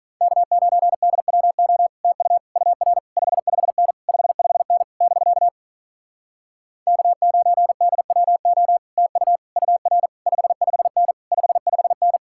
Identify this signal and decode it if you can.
Morse code, heard as a keyed tone, about 46 wpm, 700 Hz. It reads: K9DWO TU UR 55N 55N <BK> K9DWO TU UR 55N 55N